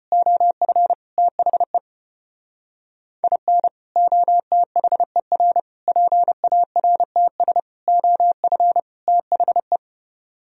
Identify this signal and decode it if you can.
Morse code, heard as a keyed tone, about 33 wpm, 700 Hz. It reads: OF THE IN OTHER PARTS OF THE